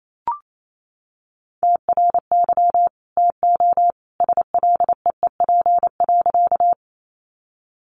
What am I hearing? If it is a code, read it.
Morse code, heard as a keyed tone, about 28 wpm, 700 Hz. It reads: TRY TO SLEEP.
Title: Sentences from Top 500 Words 28wpm